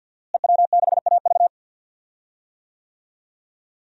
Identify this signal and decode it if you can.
Morse code, heard as a keyed tone, about 52 wpm, 700 Hz. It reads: EW6AV
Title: DXCC-100 Call Signs - 5 Characters - Repeated 2x - 50wpm